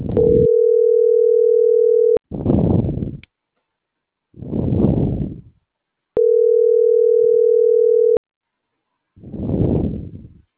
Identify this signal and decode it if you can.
Ringback tone (North America), 440 + 480 Hz.
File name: CENTRAL MAERSK / out gabrielle.melendez - .25809